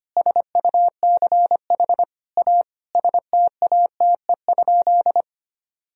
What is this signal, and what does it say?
Morse code, heard as a keyed tone, about 25 wpm, 700 Hz. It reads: SUCH A STATE?
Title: Sentences from Top 1000 Words - Review Part 2 - 25wpm